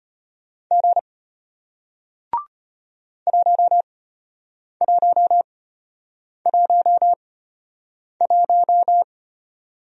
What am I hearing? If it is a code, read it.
Morse code, heard as a keyed tone, about 33 wpm, spaced as for 8 wpm, 700 Hz. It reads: G 1111